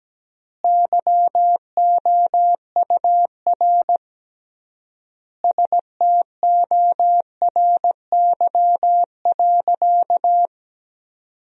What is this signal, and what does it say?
Morse code, heard as a keyed tone, about 17 wpm, 700 Hz. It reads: YOUR STORY.
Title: 3x Word-Spacing - Sentences from Top 300 Words - 17wpm